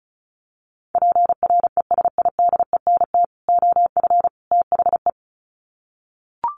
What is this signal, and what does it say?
Morse code, heard as a keyed tone, about 34 wpm, 700 Hz. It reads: PRESIDENT OF THE
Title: Sets of 3 Words 35wpm